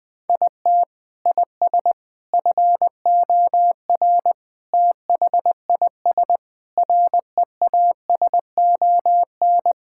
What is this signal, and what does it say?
Morse code, heard as a keyed tone, about 20 wpm, 700 Hz. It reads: IT IS FOR THIS REASON
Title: Sentences with 6-Word Phrases - Spoken Beginning - 20wpm